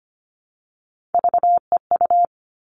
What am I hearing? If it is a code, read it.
Morse code, heard as a keyed tone, about 25 wpm, 700 Hz. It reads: VEU
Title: All Cut Numbers - 3-Digit 25wpm